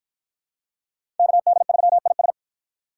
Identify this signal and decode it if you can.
Morse code, heard as a keyed tone, about 53 wpm, 700 Hz. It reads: KD3IS